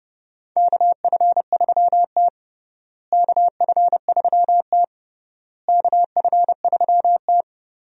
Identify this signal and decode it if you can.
Morse code, heard as a keyed tone, about 30 wpm, 700 Hz. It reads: KF3T KF3T KF3T
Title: USA Call Signs - 4 Characters - Repeated 3x - 30wpm